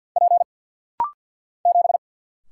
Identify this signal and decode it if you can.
Morse code, heard as a keyed tone, about 49 wpm, 700 Hz. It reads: P 7
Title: Single Letter-Number - Rapid-Fire 50wpm